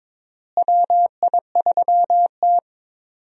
Morse code, heard as a keyed tone, about 22 wpm, 700 Hz. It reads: WI3T
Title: USA Call Signs - 4 Characters - Repeated 3x - 22wpm